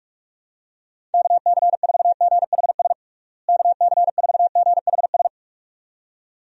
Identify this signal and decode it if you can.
Morse code, heard as a keyed tone, about 46 wpm, 700 Hz. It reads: KC4GHS KC4GHS
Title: USA Call Signs - 6 Characters - Repeated 2x - 45wpm